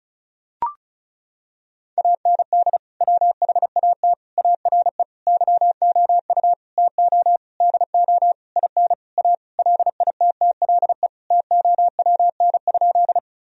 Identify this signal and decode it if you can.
Morse code, heard as a keyed tone, about 34 wpm, 700 Hz. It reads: AND WHAT ARE YOU TO DO IN A LITTLE TOWN?
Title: Sentences from Top 1000 Words - Review Part 2 - 35wpm